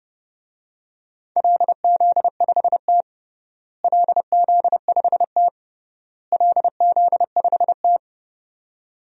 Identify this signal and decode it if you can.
Morse code, heard as a keyed tone, about 30 wpm, 700 Hz. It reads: LZ5T LZ5T LZ5T
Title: DXCC-100 Call Signs - 4 Characters - Repeated 3x - 30wpm